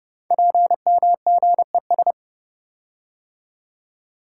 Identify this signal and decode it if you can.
Morse code, heard as a keyed tone, about 30 wpm, 700 Hz. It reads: PMGES
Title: Code Groups - 5 Letters - 30wpm